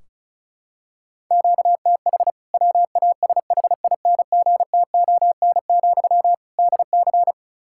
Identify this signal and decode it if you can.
Morse code, heard as a keyed tone, about 34 wpm, 700 Hz. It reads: QTH WASHINGTON, DC
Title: K1USN SST QSO Practice 35wpm